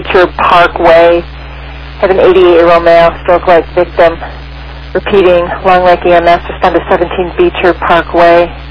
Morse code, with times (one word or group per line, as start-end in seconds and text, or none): none